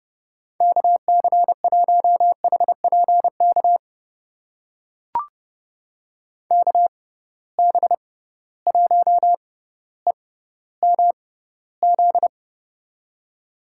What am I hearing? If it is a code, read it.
Morse code, heard as a keyed tone, about 30 wpm, spaced as for 15 wpm, 700 Hz. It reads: KC1HPK KB1EMZ
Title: USA Call Signs - 6 Characters - 6x Character Spacing - 30wpm